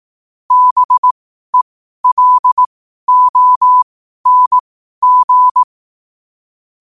Morse code, heard as a keyed tone, about 18 wpm, spaced as for 12 wpm, 1000 Hz. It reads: BELONG
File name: qotdfiles / 12wpm